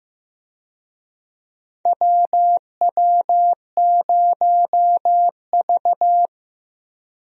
Morse code, heard as a keyed tone, about 15 wpm, 700 Hz. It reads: WW0V
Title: USA Call Signs - 4 Characters - 15wpm